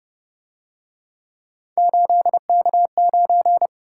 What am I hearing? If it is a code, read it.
Morse code, heard as a keyed tone, about 29 wpm, 700 Hz. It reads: 8K9